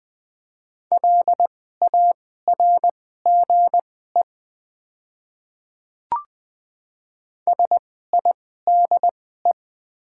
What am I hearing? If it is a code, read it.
Morse code, heard as a keyed tone, about 20 wpm, spaced as for 14 wpm, 700 Hz. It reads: LARGE SIDE
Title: 2x Character Spacing - Top 200 Words - 20wpm